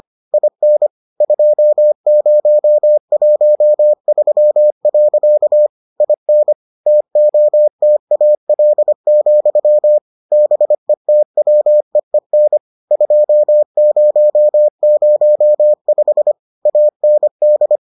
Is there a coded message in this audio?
Yes: Morse code from 0.3 to 17.8 s, heard as a keyed tone, about 25 wpm, 600 Hz. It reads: IN 2013. IN TOTAL, BETWEEN 2005 AND